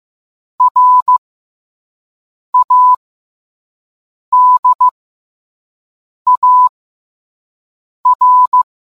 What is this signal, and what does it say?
Morse code, heard as a keyed tone, about 15 wpm, spaced as for 5 wpm, 1000 Hz. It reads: RADAR